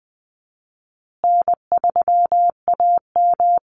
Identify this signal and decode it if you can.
Morse code, heard as a keyed tone, about 20 wpm, 700 Hz. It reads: N3AM
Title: Easy Call Signs Repeated 3x - 20wpm